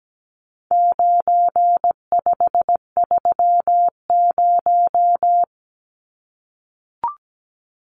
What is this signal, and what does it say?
Morse code, heard as a keyed tone, about 17 wpm, 700 Hz. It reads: 9530
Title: Numbers - 4-Digit 17wpm